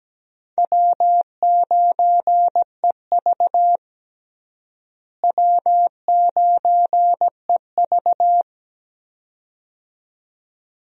Morse code, heard as a keyed tone, about 17 wpm, 700 Hz. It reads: W9EV W9EV
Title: USA Call Signs - 4 Characters - Repeated 2x - 17wpm